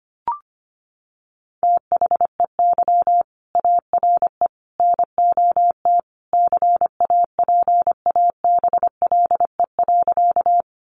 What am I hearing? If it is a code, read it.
Morse code, heard as a keyed tone, about 25 wpm, 700 Hz. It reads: THEY ARE NOT CAPABLE.
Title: Sentences from Top 1200 Words 25wpm